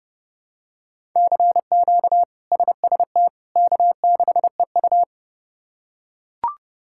Morse code, heard as a keyed tone, about 32 wpm, 700 Hz. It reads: CQ SST K6EU
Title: K1USN SST QSO Practice - 30wpm